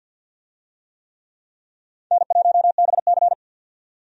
Morse code, heard as a keyed tone, about 47 wpm, 700 Hz. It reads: N1BC
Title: USA Call Signs - 4 Characters - 50wpm